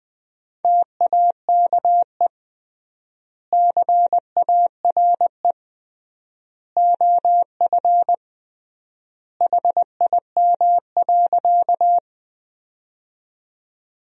Morse code, heard as a keyed tone, about 21 wpm, 700 Hz. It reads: TAKE CARE OF HIM.
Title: 3x Word-Spacing - Sentences from Top 300 Words - 20wpm